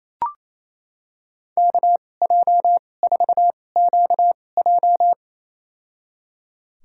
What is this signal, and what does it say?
Morse code, heard as a keyed tone, about 28 wpm, spaced as for 20 wpm, 700 Hz. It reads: KJ4QJ
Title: USA Call Signs - 2x Character Spacing - 28wpm